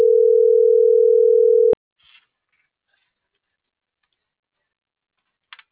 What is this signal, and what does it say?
Ringback tone (North America), 440 + 480 Hz.